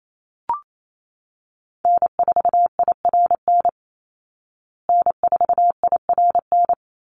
Morse code, heard as a keyed tone, about 28 wpm, 700 Hz. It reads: N4IRN N4IRN
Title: Call Signs Repeated 3x - 28wpm